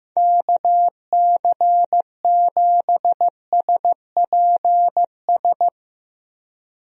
Morse code, heard as a keyed tone, about 15 wpm, 700 Hz. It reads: KC7SPS
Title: USA Call Signs - 6 Characters - Repeated 3x - 15wpm